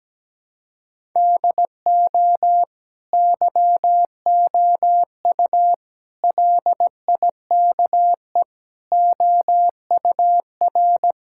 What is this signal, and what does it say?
Morse code, heard as a keyed tone, about 17 wpm, 700 Hz. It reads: DO YOU LIKE OUR
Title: Sentences from Top 1000 Words - Review Part 6 - 17wpm